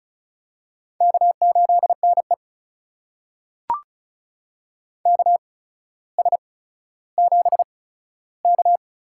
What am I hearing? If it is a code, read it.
Morse code, heard as a keyed tone, about 36 wpm, spaced as for 15 wpm, 700 Hz. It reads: K8NE KS7K
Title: USA Call Signs - 4 Characters - 8x Character Spacing - 35wpm